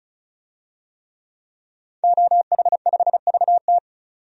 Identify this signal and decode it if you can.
Morse code, heard as a keyed tone, about 36 wpm, 700 Hz. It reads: OH5VT
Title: DXCC-100 Call Signs - 5 Characters - 35wpm